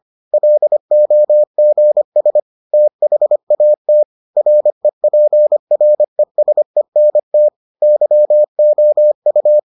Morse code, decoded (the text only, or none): LOGS THAT REPRESENT YOU